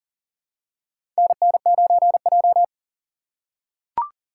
Morse code, heard as a keyed tone, about 38 wpm, 700 Hz. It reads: NN9J
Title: USA Call Signs - 4 Characters - 4x Character Spacing - 40wpm